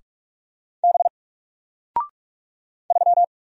Morse code, heard as a keyed tone, about 44 wpm, 700 Hz. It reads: B 3
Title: Single Character (Letter:Number:Punctuation:Prosign) Alt 45wpm